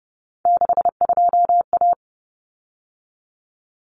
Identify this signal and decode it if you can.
Morse code, heard as a keyed tone, about 30 wpm, 700 Hz. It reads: 62A
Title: Common Cut Numbers - 3-Digit 30wpm